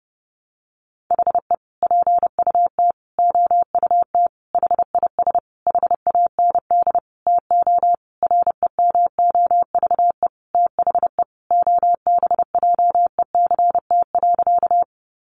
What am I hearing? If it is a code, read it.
Morse code, heard as a keyed tone, about 30 wpm, 700 Hz. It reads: HE PUT OUT HIS HAND TO REMOVE THE OBJECT.
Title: Sentences from Top 800 Words 30wpm